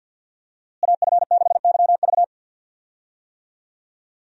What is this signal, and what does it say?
Morse code, heard as a keyed tone, about 52 wpm, 700 Hz. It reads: AF6YV
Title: USA Call Signs - 5 Characters - Repeated 2x - 50wpm